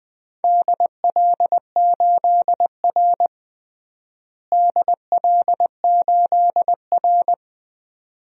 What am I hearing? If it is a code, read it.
Morse code, heard as a keyed tone, about 21 wpm, 700 Hz. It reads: DL8R DL8R
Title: DXCC-100 Call Signs - 4 Characters - Repeated 2x - 20wpm